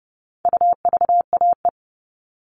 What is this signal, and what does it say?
Morse code, heard as a keyed tone, about 30 wpm, 700 Hz. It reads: UVAE
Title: All Cut Numbers - 4-Digit 30wpm